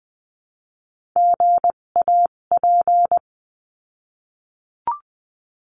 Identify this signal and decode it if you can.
Morse code, heard as a keyed tone, about 20 wpm, spaced as for 17 wpm, 700 Hz. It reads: GAP
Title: 257 Lesson 42b - Words 14-20wpm